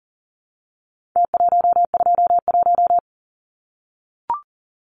Morse code, heard as a keyed tone, about 40 wpm, 700 Hz. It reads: T121
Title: Numbers - Contest Style - 4-Digit 40wpm